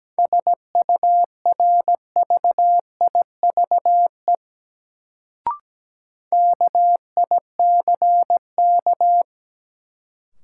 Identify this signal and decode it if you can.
Morse code, heard as a keyed tone, about 17 wpm, 700 Hz. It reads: SURVIVE KICK
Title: Sentences from Top 1300 Words - New Word Drill - 17wpm